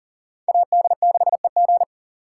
Morse code, heard as a keyed tone, about 50 wpm, 700 Hz. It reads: AD6EG
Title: Speed Racing - USA 2x2 Call Signs 40wpm